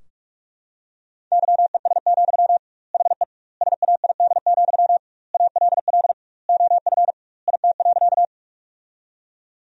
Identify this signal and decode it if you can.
Morse code, heard as a keyed tone, about 46 wpm, 700 Hz. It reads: YES, HE SAID, ALL OF IT.
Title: 2x Word-Spacing - Sentences from Top 500 Words - 45wpm